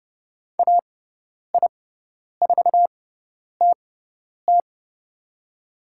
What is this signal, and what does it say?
Morse code, heard as a keyed tone, about 31 wpm, spaced as for 10 wpm, 690 Hz. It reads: AI4TT
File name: Lesson 17g - Letter u - Call Signs 30wpm